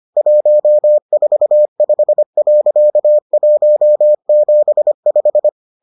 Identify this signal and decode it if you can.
Morse code, heard as a keyed tone, about 25 wpm, 600 Hz. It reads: 145.175